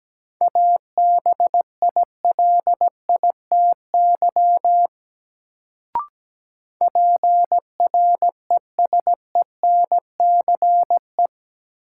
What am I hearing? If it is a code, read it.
Morse code, heard as a keyed tone, about 17 wpm, 700 Hz. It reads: ABILITY PRESENCE
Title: Sets of 3 Words - New Word Drill - 17wpm